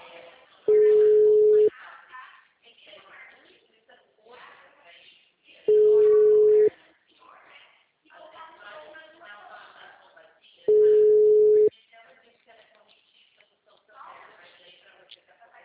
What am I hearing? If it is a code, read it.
Ringback tone (Europe, 425 Hz), with speech.